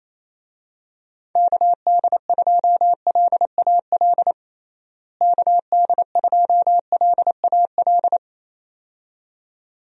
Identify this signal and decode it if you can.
Morse code, heard as a keyed tone, about 28 wpm, 700 Hz. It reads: KD2LAL KD2LAL
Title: USA Call Signs - 6 Characters - Repeated 2x - 28wpm